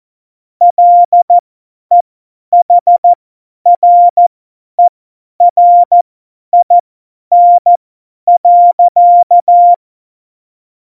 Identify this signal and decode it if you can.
Morse code, heard as a keyed tone, about 14 wpm, spaced as for 10 wpm, 700 Hz. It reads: LEHRERIN.
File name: Morsekurs / Witze / Witz 23. April 2019, 50 F70